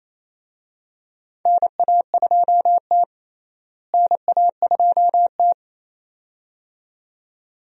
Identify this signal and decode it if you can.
Morse code, heard as a keyed tone, about 27 wpm, 700 Hz. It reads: NA2T NA2T
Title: USA Call Signs - 4 Characters - Repeated 2x - 28wpm